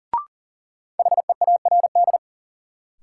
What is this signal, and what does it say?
Morse code, heard as a keyed tone, about 40 wpm, 700 Hz. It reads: HEARD
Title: Sentences from Top 400 Words - New Word Drill - 40wpm